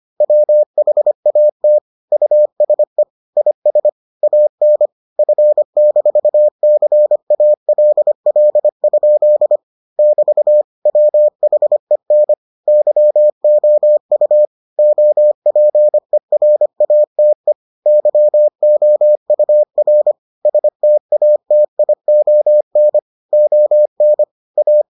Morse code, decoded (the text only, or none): WHAT USE IS AN F-CALL? <BT> WHEN YOU OPERATE YOUR STATION ON A